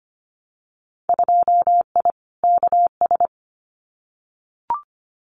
Morse code, heard as a keyed tone, about 25 wpm, 700 Hz. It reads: 2I KS
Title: Winter Field Day Exchanges Encore 25wpm